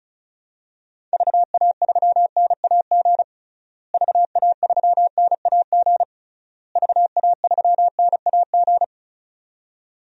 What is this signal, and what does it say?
Morse code, heard as a keyed tone, about 34 wpm, 700 Hz. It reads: VA3NAG VA3NAG VA3NAG
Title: DXCC-100 Call Signs - 6 Characters - Repeated 3x - 35wpm